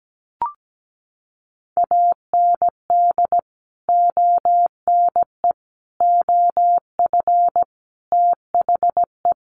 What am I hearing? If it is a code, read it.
Morse code, heard as a keyed tone, about 17 wpm, 700 Hz. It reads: AND ONE OF THE
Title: Sets of 5 Words 17wpm